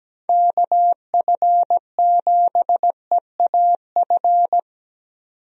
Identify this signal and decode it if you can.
Morse code, heard as a keyed tone, about 17 wpm, 700 Hz. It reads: KF7EAF
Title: USA Call Signs - 6 Characters - 17wpm